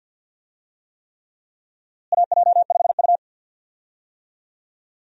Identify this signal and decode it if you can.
Morse code, heard as a keyed tone, about 50 wpm, 700 Hz. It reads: AJ5U
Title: USA Call Signs - 4 Characters - Repeated 3x - 50wpm